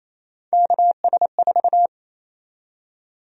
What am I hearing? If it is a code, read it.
Morse code, heard as a keyed tone, about 28 wpm, 700 Hz. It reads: KS4
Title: USA Call Sign Prefixes - 28wpm